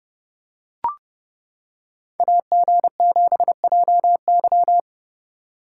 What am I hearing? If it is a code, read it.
Morse code, heard as a keyed tone, about 29 wpm, 700 Hz. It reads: AG7JY